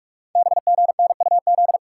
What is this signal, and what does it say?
Morse code, heard as a keyed tone, about 44 wpm, 700 Hz. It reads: BGNUZ
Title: Code Groups - 5 Letters - 45wpm